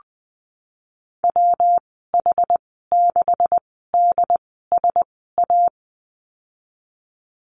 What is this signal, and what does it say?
Morse code, heard as a keyed tone, about 20 wpm, spaced as for 15 wpm, 700 Hz. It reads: WH6DSA
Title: 147 Lesson 24g - Letter y - Call Signs 20wpm